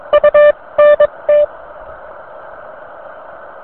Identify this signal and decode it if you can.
Morse code, heard as a keyed tone, about 22 wpm, spaced as for 19 wpm, 590 Hz. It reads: U NT